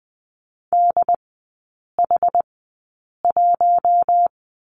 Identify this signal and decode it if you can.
Morse code, heard as a keyed tone, about 20 wpm, spaced as for 8 wpm, 700 Hz. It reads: DH1
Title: 074 Lesson 13e - Letter L - Three Characters 20wpm